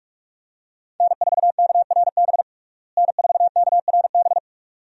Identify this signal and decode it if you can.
Morse code, heard as a keyed tone, about 46 wpm, 700 Hz. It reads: N4KRB N4KRB
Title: USA Call Signs - 5 Characters - Repeated 2x - 45wpm